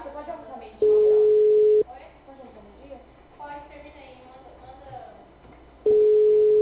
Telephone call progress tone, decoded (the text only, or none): ringback tone